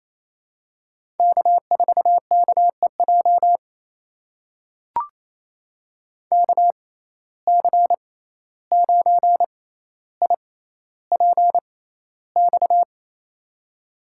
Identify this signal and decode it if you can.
Morse code, heard as a keyed tone, about 28 wpm, spaced as for 14 wpm, 700 Hz. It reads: K4KEJ KC9IPX
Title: USA Call Signs - 6x Character Spacing - 28wpm